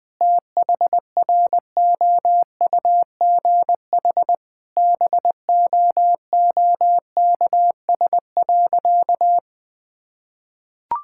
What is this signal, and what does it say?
Morse code, heard as a keyed tone, about 20 wpm, 700 Hz. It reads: THROUGH BOOKS.
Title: Difficult Sentences with Hints - 20wpm